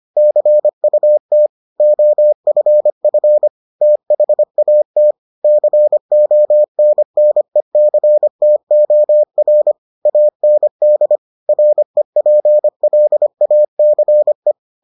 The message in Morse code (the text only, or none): CUT OFF THAT CONNECTOR AND REPLACE